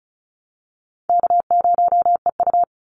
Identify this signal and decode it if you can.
Morse code, heard as a keyed tone, about 34 wpm, 700 Hz. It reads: K0EU
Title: Call Signs Repeated 3x - 35wpm